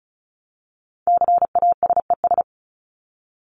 Morse code, heard as a keyed tone, about 34 wpm, 700 Hz. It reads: CASES
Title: ES:S Suffix - Top 100 Words with Base Words 35wpm